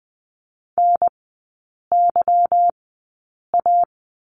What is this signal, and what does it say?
Morse code, heard as a keyed tone, about 20 wpm, spaced as for 8 wpm, 700 Hz. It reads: NYA
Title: 145 Lesson 24e - Letter y - Three Characters 20wpm